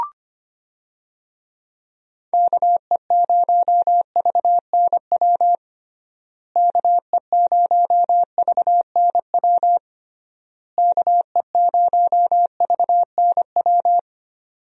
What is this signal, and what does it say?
Morse code, heard as a keyed tone, about 25 wpm, 700 Hz. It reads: KE0VNW KE0VNW KE0VNW
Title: USA Call Signs - 6 Characters - Repeated 3x - 25wpm